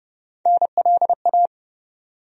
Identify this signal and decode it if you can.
Morse code, heard as a keyed tone, about 30 wpm, 700 Hz. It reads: NLA
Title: USA Call Sign Postfixes - 3 Letters 30wpm